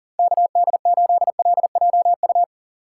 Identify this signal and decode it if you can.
Morse code, heard as a keyed tone, about 40 wpm, 700 Hz. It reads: KD8LJU
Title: USA Call Signs - 6 Characters - 6x Character Spacing - 40wpm